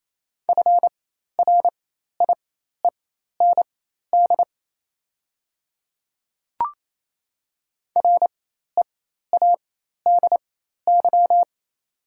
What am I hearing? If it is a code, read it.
Morse code, heard as a keyed tone, about 28 wpm, spaced as for 13 wpm, 690 Hz. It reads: FRIEND READY
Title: 4x Character Spacing - Top 500 Words - 28wpm